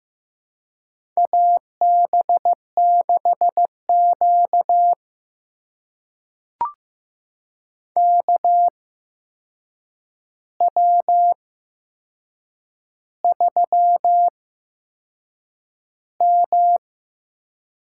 Morse code, heard as a keyed tone, about 15 wpm, spaced as for 6 wpm, 700 Hz. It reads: AB6Q KW3M
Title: USA Call Signs - 4 Characters - 8x Character Spacing - 15wpm